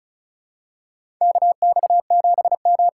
Morse code, heard as a keyed tone, about 36 wpm, 700 Hz. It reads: KX7M
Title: K1USN SST QSO Practice - 35wpm